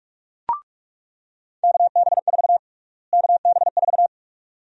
Morse code, heard as a keyed tone, about 46 wpm, 700 Hz. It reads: KB4 KB4
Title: USA Call Sign Prefixes - Repeated 2x - 45wpm